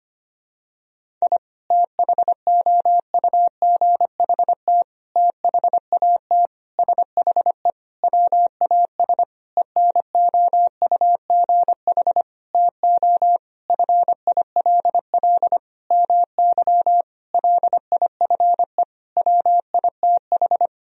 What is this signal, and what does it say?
Morse code, heard as a keyed tone, about 25 wpm, 700 Hz. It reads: I THOUGHT THAT SHE WAS ENOUGH TO FILL MY LIFE WITH
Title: Sentences from Top 1000 Words - Review Part 3 - 25wpm